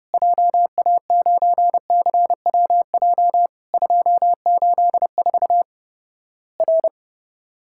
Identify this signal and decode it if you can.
Morse code, heard as a keyed tone, about 31 wpm, 700 Hz. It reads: JA9CWJ 284 R
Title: Worked All Europe DX Contest - QTC Exchange - 30wpm